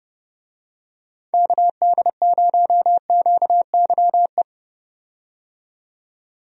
Morse code, heard as a keyed tone, about 31 wpm, 700 Hz. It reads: KD0QYE
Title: USA Call Signs - 6 Characters - 30wpm